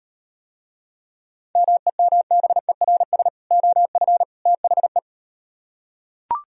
Morse code, heard as a keyed tone, about 38 wpm, 700 Hz. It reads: MEMBERS OF THE